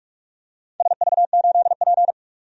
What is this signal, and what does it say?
Morse code, heard as a keyed tone, about 43 wpm, 700 Hz. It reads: SV8P
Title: DXCC-100 Call Signs - 4 Characters - 45wpm